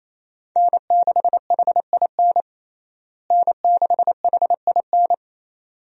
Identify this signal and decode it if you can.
Morse code, heard as a keyed tone, about 28 wpm, 700 Hz. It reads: N6HIN N6HIN
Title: USA Call Signs - 5 Characters - Repeated 2x - 28wpm